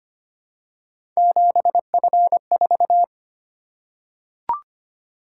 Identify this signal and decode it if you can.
Morse code, heard as a keyed tone, about 25 wpm, 700 Hz. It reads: 7F4
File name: Three Letter-Number Combinations - 25wpm